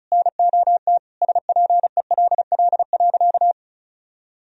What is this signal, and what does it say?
Morse code, heard as a keyed tone, about 34 wpm, 700 Hz. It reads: NOT SPELL.
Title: Sentences from Top 1000 Words - Review Part 1 - 35wpm